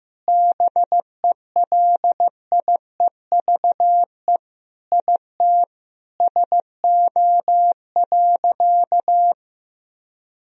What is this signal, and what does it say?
Morse code, heard as a keyed tone, about 15 wpm, 700 Hz. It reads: BELIEVE IT SO.
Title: Sentences from Top 1000 Words - Review Part 4 - 15wpm